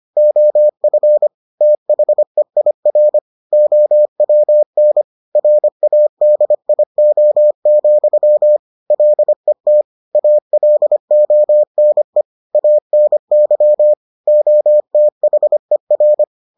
Morse code, heard as a keyed tone, about 25 wpm, 600 Hz. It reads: OF THEIR OWN RADIO, LET ALONE ANY OTHER